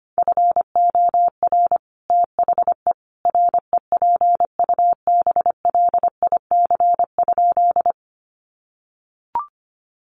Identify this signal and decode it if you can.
Morse code, heard as a keyed tone, about 25 wpm, 700 Hz. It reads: FOR THE REPUBLIC?
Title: Sentences from Top 1500 Words 25wpm